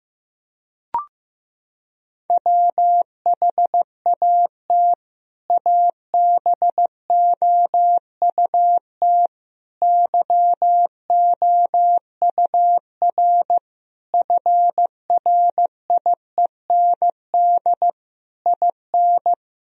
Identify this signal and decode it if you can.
Morse code, heard as a keyed tone, about 15 wpm, 700 Hz. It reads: WHAT ABOUT YOUR FRIEND IN
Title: Sentences from Top 1000 Words - Review Part 1 - 15wpm